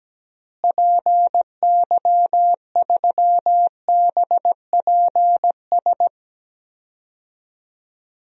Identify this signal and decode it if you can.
Morse code, heard as a keyed tone, about 17 wpm, 700 Hz. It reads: PY3BPS
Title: DXCC-100 Call Signs - 6 Characters - Repeated 2x - 17wpm